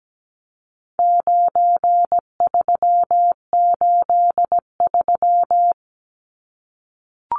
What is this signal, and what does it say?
Morse code, heard as a keyed tone, about 17 wpm, 700 Hz. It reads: 9383
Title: Numbers - 4-Digit 17wpm